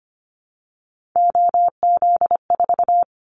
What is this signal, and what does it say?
Morse code, heard as a keyed tone, about 25 wpm, 690 Hz. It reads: OZ4
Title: Callsign Prefixes for Top 50 DXCC - 25wpm